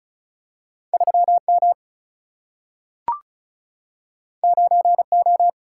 Morse code, heard as a keyed tone, about 36 wpm, 700 Hz. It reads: 3M 9O
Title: Single-Letter Single-Number 35wpm